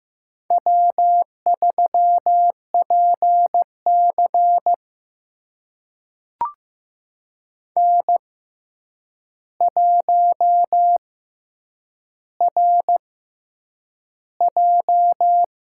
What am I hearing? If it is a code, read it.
Morse code, heard as a keyed tone, about 15 wpm, spaced as for 8 wpm, 700 Hz. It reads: W3PC N1RJ
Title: USA Call Signs - 4 Characters - 6x Character Spacing - 15wpm